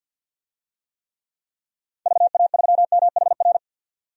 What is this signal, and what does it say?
Morse code, heard as a keyed tone, about 52 wpm, 700 Hz. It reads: VA3MHR
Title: DXCC-100 Call Signs - 6 Characters - 50wpm